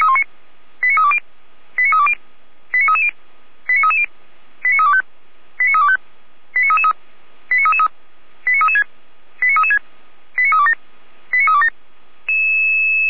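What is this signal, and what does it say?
Five-tone selective call (ZVEI), heard as quick runs of five tones, tones of about 80 ms. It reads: 89320 89320 89330 89330 89325 89325 89333 89336 89336 89327 89327
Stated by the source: ZVEI five-tone call 89315 received by SDR, logged by Feuerwehr Orlamünde.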